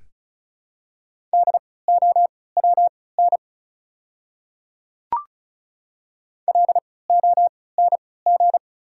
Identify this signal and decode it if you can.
Morse code, heard as a keyed tone, about 36 wpm, spaced as for 20 wpm, 700 Hz. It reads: DOWN LONG
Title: 3x Character Spacing - Top 100 Words - 35wpm